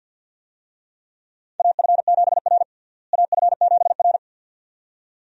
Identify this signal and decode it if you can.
Morse code, heard as a keyed tone, about 52 wpm, 700 Hz. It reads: AF7R AF7R
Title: USA Call Signs - 4 Characters - Repeated 2x - 50wpm